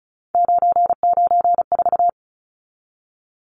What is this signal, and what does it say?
Morse code, heard as a keyed tone, about 34 wpm, 700 Hz. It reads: 994
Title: Numbers - 3-Digit 35wpm